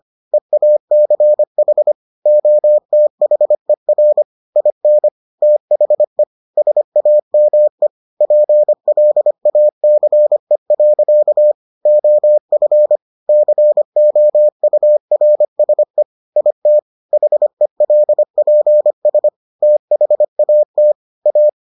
Morse code, heard as a keyed tone, about 25 wpm, 600 Hz. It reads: EACH OTHER IN THE SAME PLACE. OF COURSE IT HELPS THAT A